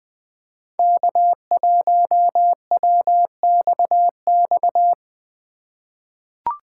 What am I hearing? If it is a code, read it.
Morse code, heard as a keyed tone, about 21 wpm, 700 Hz. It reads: K1WXX
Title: USA Call Signs - 4x Character Spacing - 20wpm